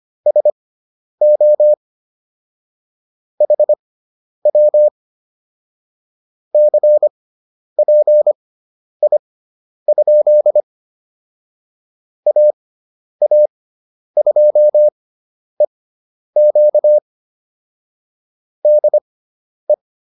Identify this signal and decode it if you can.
Morse code, heard as a keyed tone, about 25 wpm, spaced as for 10 wpm, 600 Hz. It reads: SO HW CPI? AA2EQ DE